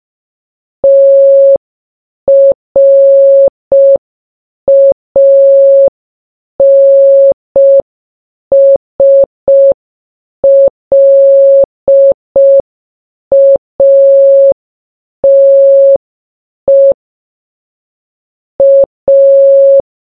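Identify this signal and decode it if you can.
Morse code, heard as a keyed tone, about 5 wpm, 550 Hz. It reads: TRANSLATE A
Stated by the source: Morse sound from the game Baba Is You.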